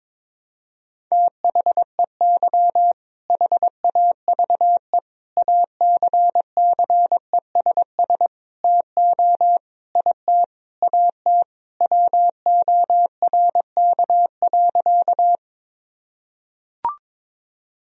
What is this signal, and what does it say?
Morse code, heard as a keyed tone, about 22 wpm, 700 Hz. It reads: THEY HAVE ACCESS TO IT AT WORK.
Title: Sentences from Top 1000 Words - Review Part 4 - 22wpm